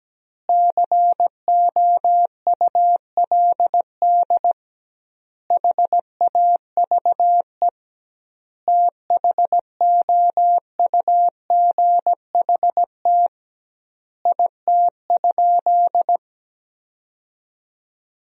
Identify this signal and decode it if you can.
Morse code, heard as a keyed tone, about 17 wpm, 700 Hz. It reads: COULD HAVE THOUGHT IT?
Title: 2x Word-Spacing - Sentences from Top 300 Words - 17wpm